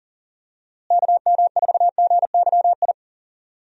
Morse code, heard as a keyed tone, about 39 wpm, 700 Hz. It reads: KM4GYI
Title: USA Call Signs - 6 Characters - Repeated 2x - 40wpm